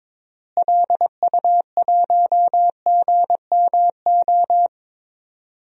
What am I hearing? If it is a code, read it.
Morse code, heard as a keyed tone, about 22 wpm, 700 Hz. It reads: LU1GMO
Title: DXCC-100 Call Signs - 6 Characters - Repeated 2x - 22wpm